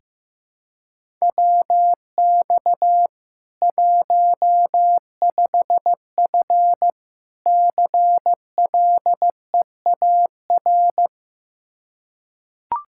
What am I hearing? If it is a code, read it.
Morse code, heard as a keyed tone, about 15 wpm, 700 Hz. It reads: WX 15F CLEAR
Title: Practice QSO - Exchange 2 - Info - 15wpm